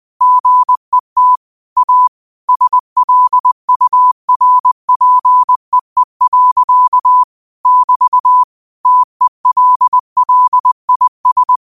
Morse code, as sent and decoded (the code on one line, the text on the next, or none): --. - / .- / ... .-.. ..- .-. .--. . .-.-.- / -...- / - . .-.. .-.. .. ...
GET A SLURPEE. <BT> TELLIS